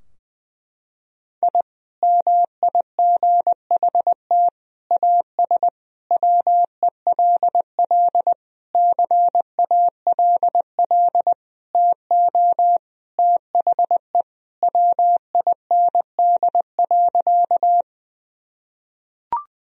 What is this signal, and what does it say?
Morse code, heard as a keyed tone, about 21 wpm, 700 Hz. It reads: I MIGHT AS WELL CALL TO THE WIND.
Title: Sentences from Top 1000 Words - Review Part 2 - 20wpm